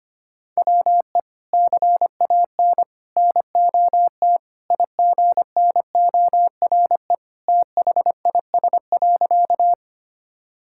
Morse code, heard as a keyed tone, about 25 wpm, 700 Hz. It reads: WE CAN NOT IGNORE THIS.
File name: practice / 25wpm / Sentences with 3-Word Phrases - 25wpm